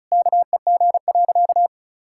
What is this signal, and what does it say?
Morse code, heard as a keyed tone, about 36 wpm, 700 Hz. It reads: KEG.
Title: Pangram Challenge v2 - 35wpm